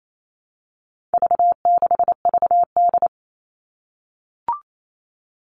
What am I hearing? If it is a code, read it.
Morse code, heard as a keyed tone, about 28 wpm, 700 Hz. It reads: V6VD
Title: All Cut Numbers - 4-Digit 28wpm